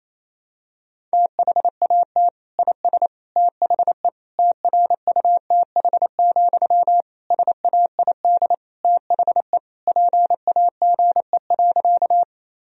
Morse code, heard as a keyed tone, about 28 wpm, 700 Hz. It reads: THAT IS THE TRUTH, SAID THE PAGE.
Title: Sentences from Top 1000 Words - Review Part 1 - 28wpm